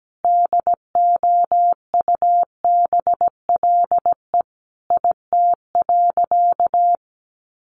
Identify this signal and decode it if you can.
Morse code, heard as a keyed tone, about 17 wpm, 700 Hz. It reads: DOUBLE IT.
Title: Sentences from Top 1400 Words 17wpm